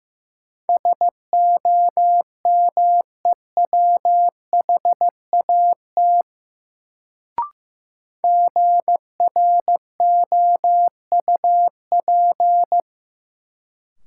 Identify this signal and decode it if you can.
Morse code, heard as a keyed tone, about 15 wpm, 700 Hz. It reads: SOMEWHAT GROUP
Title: Sentences from Top 1000 Words - Review Part 1 - New Word Drill - 15wpm